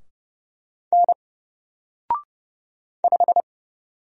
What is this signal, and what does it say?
Morse code, heard as a keyed tone, about 30 wpm, 700 Hz. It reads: N 5
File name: Single-Letter-Number-30wpm